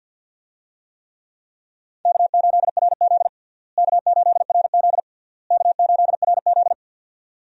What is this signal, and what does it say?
Morse code, heard as a keyed tone, about 52 wpm, 700 Hz. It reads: K8RZ K8RZ K8RZ
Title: USA Call Signs - 4 Characters - Repeated 3x - 50wpm